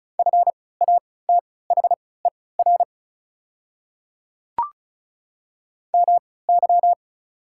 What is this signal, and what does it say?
Morse code, heard as a keyed tone, about 36 wpm, spaced as for 20 wpm, 700 Hz. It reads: FATHER MY
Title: 3x Character Spacing - Top 200 Words - 35wpm